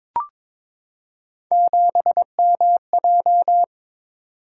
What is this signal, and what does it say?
Morse code, heard as a keyed tone, about 22 wpm, 700 Hz. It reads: 7MJ
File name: practice / 22wpm / Three Letter-Number Combinations - 22wpm